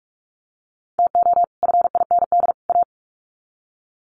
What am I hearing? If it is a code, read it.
Morse code, heard as a keyed tone, about 44 wpm, 700 Hz. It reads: TO FIND A
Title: Sets of 3 Words 45wpm